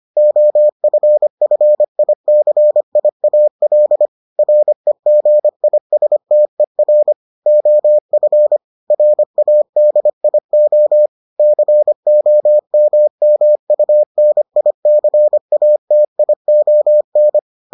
Morse code, heard as a keyed tone, about 25 wpm, 600 Hz. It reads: OFFICIAL REGISTER OF RADIO COMMUNICATION